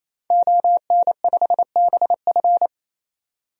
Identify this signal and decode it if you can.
Morse code, heard as a keyed tone, about 28 wpm, 700 Hz. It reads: ON5BF
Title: DXCC-100 Call Signs - 5 Characters - 28wpm